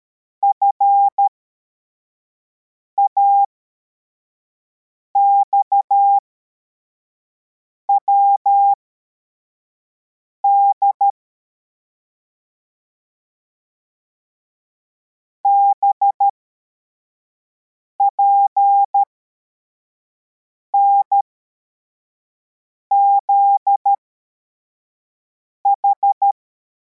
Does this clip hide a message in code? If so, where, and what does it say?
Yes: Morse code from 0.4 to 26.3 s, heard as a keyed tone, about 13 wpm, spaced as for 4 wpm, 800 Hz. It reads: FAXWD BPNZH